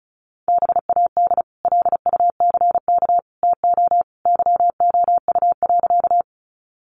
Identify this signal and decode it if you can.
Morse code, heard as a keyed tone, about 34 wpm, 700 Hz. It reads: BAD LUCK TO YOU.
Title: Sentences from Top 1500 Words 35wpm